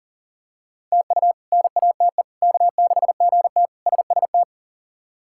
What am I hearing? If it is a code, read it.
Morse code, heard as a keyed tone, about 40 wpm, 700 Hz. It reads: TU NATE K6GT SST
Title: K1USN SST QSO Practice - 40wpm